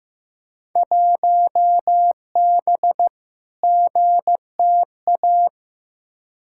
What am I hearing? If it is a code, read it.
Morse code, heard as a keyed tone, about 15 wpm, 700 Hz. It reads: 1B GTA
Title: Field Day Exchanges Encore - 15wpm